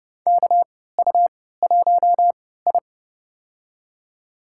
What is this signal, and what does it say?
Morse code, heard as a keyed tone, about 29 wpm, spaced as for 17 wpm, 700 Hz. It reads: KU1I
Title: USA Call Signs - 3x Character Spacing - 30wpm